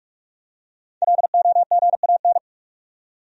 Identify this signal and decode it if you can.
Morse code, heard as a keyed tone, about 44 wpm, 700 Hz. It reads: LOGAN
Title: USA Common Names - Max 5 Letters - 45wpm